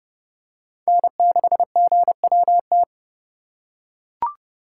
Morse code, heard as a keyed tone, about 29 wpm, 700 Hz. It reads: N6GWT
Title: USA Call Signs - 5 Characters - 4x Character Spacing - 30wpm